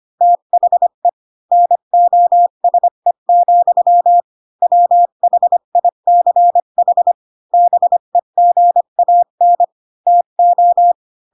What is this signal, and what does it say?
Morse code, heard as a keyed tone, about 25 wpm, 700 Hz. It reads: THE NOSE, WHICH BEGAN TO
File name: Pinocchio / Pinocchio ch3 20 25 ik1hge